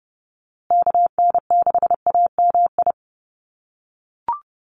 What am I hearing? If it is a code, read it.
Morse code, heard as a keyed tone, about 30 wpm, 700 Hz. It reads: KN6AMI
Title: USA Call Signs Repeated 3x 30wpm